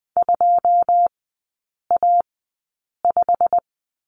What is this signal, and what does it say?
Morse code, heard as a keyed tone, about 20 wpm, spaced as for 8 wpm, 700 Hz. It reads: 2A5
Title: 152 Lesson 25e - Letter p - Three Characters 20wpm